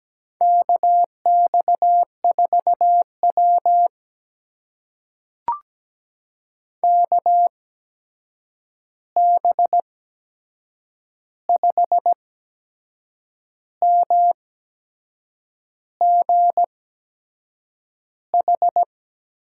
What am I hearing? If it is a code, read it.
Morse code, heard as a keyed tone, about 17 wpm, spaced as for 6 wpm, 700 Hz. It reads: KX4W KB5MGH
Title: USA Call Signs - 8x Character Spacing - 17wpm